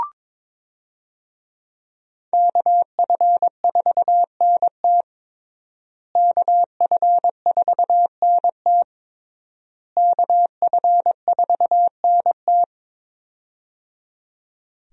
Morse code, heard as a keyed tone, about 21 wpm, 700 Hz. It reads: KF4NT KF4NT KF4NT
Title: USA Call Signs - 5 Characters - Repeated 3x - 22wpm